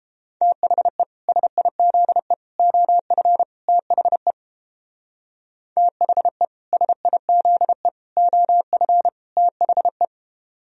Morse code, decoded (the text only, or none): THE SIZE OF THE THE SIZE OF THE